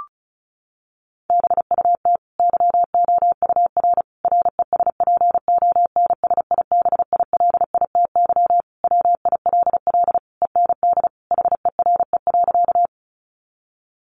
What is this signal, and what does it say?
Morse code, heard as a keyed tone, about 34 wpm, 700 Hz. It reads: BUT YOUR RESPONSIBILITY WILL END HERE.
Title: Sentences from Top 1200 Words 35wpm